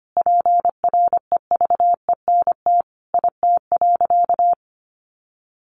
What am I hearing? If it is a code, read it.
Morse code, heard as a keyed tone, about 25 wpm, 700 Hz. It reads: PREVENT IT.
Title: Sentences from Top 1100 Words 25wpm